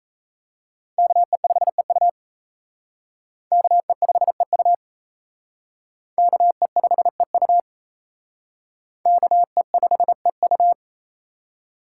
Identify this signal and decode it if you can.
Morse code, heard as a keyed tone, about 37 wpm, 700 Hz. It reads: KE5EU KE5EU KE5EU KE5EU